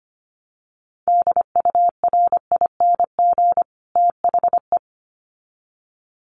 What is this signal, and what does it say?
Morse code, heard as a keyed tone, about 25 wpm, 700 Hz. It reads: DURING THE
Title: Sets of 2 Words 25wpm